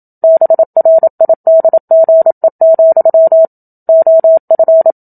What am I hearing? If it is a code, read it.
Morse code, heard as a keyed tone, about 27 wpm, 650 Hz. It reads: BRIDGE, OF
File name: Book Two / 01-Chapter 1 - 27 WPM